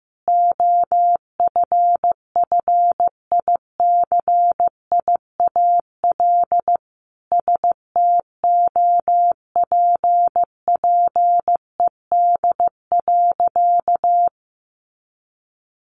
Morse code, heard as a keyed tone, about 15 wpm, 700 Hz. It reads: OFFICIAL STOPPED.
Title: A Sound Of Thunder 15wpm